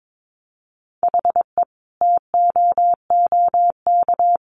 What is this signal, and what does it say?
Morse code, heard as a keyed tone, about 22 wpm, 700 Hz. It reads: HE TOOK
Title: Sentences from Top 1500 Words 22wpm